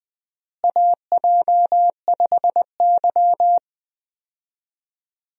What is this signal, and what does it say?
Morse code, heard as a keyed tone, about 21 wpm, 700 Hz. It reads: AJ5Y
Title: USA Call Signs - 4 Characters - Repeated 3x - 20wpm